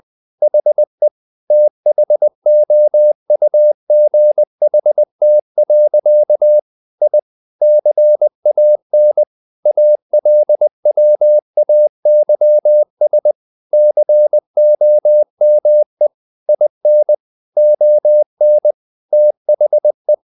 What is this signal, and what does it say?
Morse code, heard as a keyed tone, about 20 wpm, 600 Hz. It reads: HE THOUGHT. I CAN ALWAYS COME IN ON THE
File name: the old man and the sea / the old man and the sea part 1 20wpm